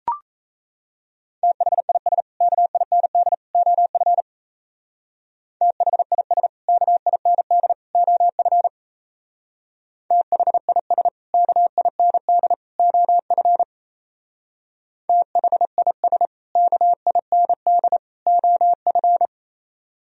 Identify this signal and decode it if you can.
Morse code, heard as a keyed tone, about 33 wpm, 700 Hz. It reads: THIS KIND OF THIS KIND OF THIS KIND OF THIS KIND OF
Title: Speed-Racing - Sets of 3 Words 28wpm